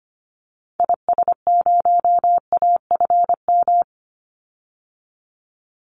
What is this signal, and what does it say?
Morse code, heard as a keyed tone, about 25 wpm, 700 Hz. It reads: IS0AFM
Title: Call Signs - 25wpm